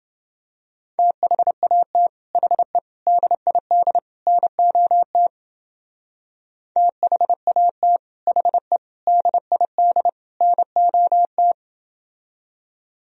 Morse code, heard as a keyed tone, about 29 wpm, 700 Hz. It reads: THAT HE DID NOT THAT HE DID NOT